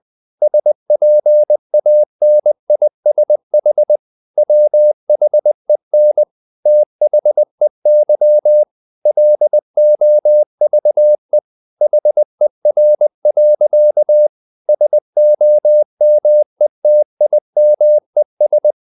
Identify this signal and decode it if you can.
Morse code, heard as a keyed tone, about 20 wpm, 600 Hz. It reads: SPANISH WHEN THEY LOVE HER. SOMETIMES
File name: the old man and the sea / the old man and the sea part 1 20wpm